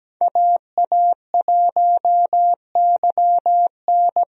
Morse code, heard as a keyed tone, about 17 wpm, 700 Hz. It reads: AA1YN
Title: USA Call Signs - 5 Characters - 17wpm